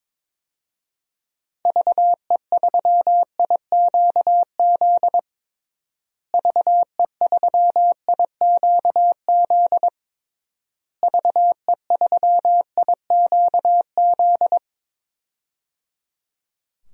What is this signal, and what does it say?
Morse code, heard as a keyed tone, about 21 wpm, 700 Hz. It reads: VE3IQZ VE3IQZ VE3IQZ
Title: DXCC-100 Call Signs - 6 Characters - Repeated 3x - 22wpm